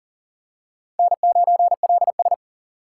Morse code, heard as a keyed tone, about 37 wpm, 700 Hz. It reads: N9LS
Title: USA Call Signs - 4 Characters - 4x Character Spacing - 40wpm